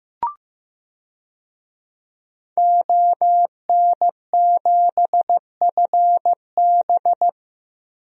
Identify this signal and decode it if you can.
Morse code, heard as a keyed tone, about 15 wpm, 700 Hz. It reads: ON7FB
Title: DXCC-100 Call Signs - 5 Characters - Repeated 3x - 15wpm